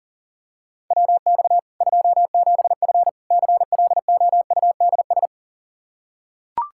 Morse code, heard as a keyed tone, about 37 wpm, 700 Hz. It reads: WX 27F CLOUDS
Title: K1USN SST QSO Practice 40wpm